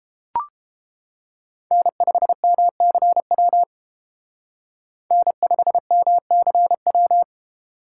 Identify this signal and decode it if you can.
Morse code, heard as a keyed tone, about 32 wpm, 700 Hz. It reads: N5MCW N5MCW